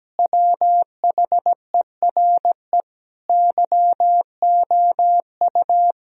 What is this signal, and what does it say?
Morse code, heard as a keyed tone, about 17 wpm, 700 Hz. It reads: WHERE YOU
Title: Sentences from Top 1000 Words - Review Part 2 - 17wpm